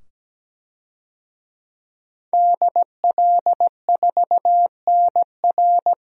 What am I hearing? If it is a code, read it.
Morse code, heard as a keyed tone, about 17 wpm, 700 Hz. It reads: DL4NR
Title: DXCC-100 Call Signs - 5 Characters - 17wpm